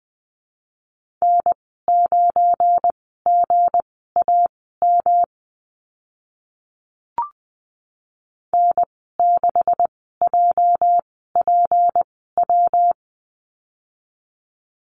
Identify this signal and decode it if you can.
Morse code, heard as a keyed tone, about 20 wpm, spaced as for 14 wpm, 700 Hz. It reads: N9GAM N6JPW
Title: 238 Lesson 38g - Letter q - Call Signs 20wpm